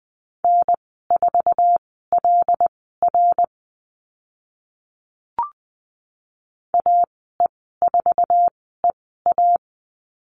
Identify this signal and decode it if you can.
Morse code, heard as a keyed tone, about 20 wpm, spaced as for 14 wpm, 700 Hz. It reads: N4LR AE4EA
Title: 076 Lesson 13g - Letter L - Call Signs 20wpm